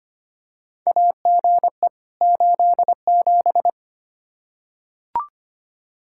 Morse code, heard as a keyed tone, about 25 wpm, 700 Hz. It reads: AGE 87
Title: K1USN SST QSO Practice 25wpm